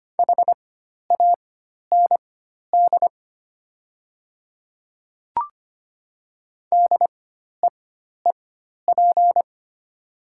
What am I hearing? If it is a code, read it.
Morse code, heard as a keyed tone, about 25 wpm, spaced as for 12 wpm, 690 Hz. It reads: HAND DEEP
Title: 4x Character Spacing - Top 500 Words - 25wpm